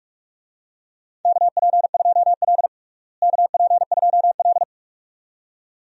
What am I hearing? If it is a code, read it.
Morse code, heard as a keyed tone, about 46 wpm, 700 Hz. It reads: KP2L KP2L
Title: DXCC-100 Call Signs - 4 Characters - Repeated 2x - 45wpm